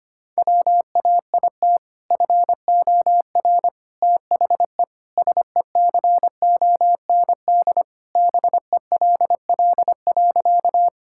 Morse code, heard as a keyed tone, about 25 wpm, 700 Hz. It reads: WAIT FOR THE SECOND BELL.
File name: practice / 25wpm / Sentences with 2-Word Phrases - 25wpm